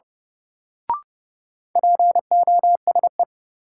Morse code, heard as a keyed tone, about 31 wpm, 700 Hz. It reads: POSE
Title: Sentences from Top 1000 Words - Review Part 2 - New Word Drill - 30wpm